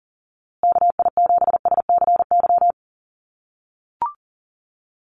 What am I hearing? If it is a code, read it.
Morse code, heard as a keyed tone, about 40 wpm, 700 Hz. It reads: KI7SCY
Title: USA Call Signs Repeated 3x 40wpm